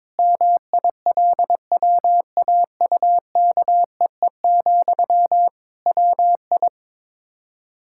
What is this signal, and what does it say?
Morse code, heard as a keyed tone, about 22 wpm, 700 Hz. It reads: MILWAUKEE, WI
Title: Practice QSO - Exchange 1 - Beginning - 22wpm